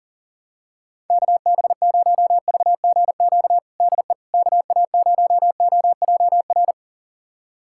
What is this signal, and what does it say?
Morse code, heard as a keyed tone, about 40 wpm, 700 Hz. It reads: KB0VGQ DE KA0OJR
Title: USA Callsign DE Callsign 40wpm